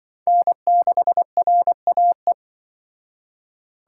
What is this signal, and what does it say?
Morse code, heard as a keyed tone, about 25 wpm, 700 Hz. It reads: N6RAE